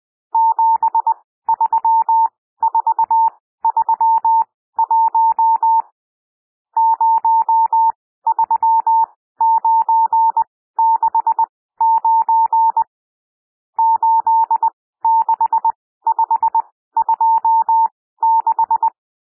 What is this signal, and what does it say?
Morse code, heard as a keyed tone, about 20 wpm, spaced as for 14 wpm, 900 Hz. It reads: 73431 03969 86526
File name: cwmp3 / N1 SOUND NUMBERS 20 wpm